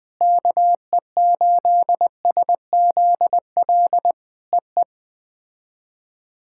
Morse code, heard as a keyed tone, about 20 wpm, 700 Hz. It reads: KE8SZL EE
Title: POTA QSO Practice - 20wpm